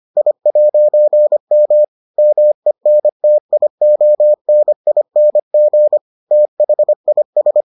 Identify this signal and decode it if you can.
Morse code, heard as a keyed tone, about 25 wpm, 600 Hz. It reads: I'M MENTIONING THIS